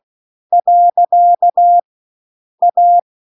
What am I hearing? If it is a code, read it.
Morse code, heard as a keyed tone, about 16 wpm, 700 Hz. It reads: . A